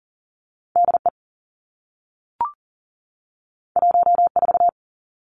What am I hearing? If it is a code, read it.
Morse code, heard as a keyed tone, about 38 wpm, 700 Hz. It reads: D E 1 4
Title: Two Challenging Letter-Number Combinations 40wpm